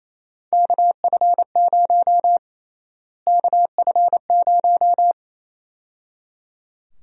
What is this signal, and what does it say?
Morse code, heard as a keyed tone, about 28 wpm, 700 Hz. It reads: KF0 KF0
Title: USA Call Sign Prefixes - Repeated 2x - 28wpm